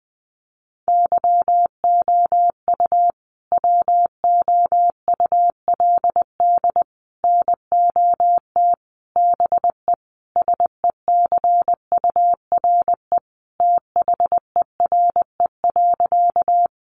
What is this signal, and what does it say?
Morse code, heard as a keyed tone, about 20 wpm, 700 Hz. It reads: YOU WOULD NOT BE SECURE THERE.
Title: Sentences from Top 1200 Words 20wpm